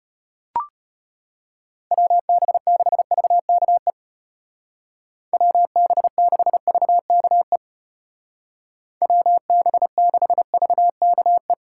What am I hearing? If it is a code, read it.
Morse code, heard as a keyed tone, about 34 wpm, 700 Hz. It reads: WB6VKE WB6VKE WB6VKE